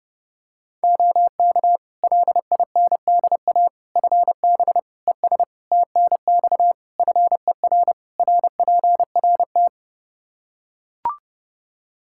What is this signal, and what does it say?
Morse code, heard as a keyed tone, about 30 wpm, 700 Hz. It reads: OK LINDA FB ES TNX FER RPRT
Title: Practice QSO - Exchange 2 - Info - 30wpm